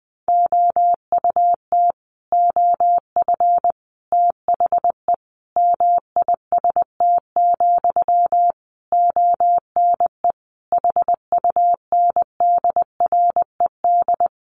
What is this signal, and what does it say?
Morse code, heard as a keyed tone, about 20 wpm, 700 Hz. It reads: OUT OF THE MIST, ONE HUNDRED
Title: A Sound Of Thunder 20wpm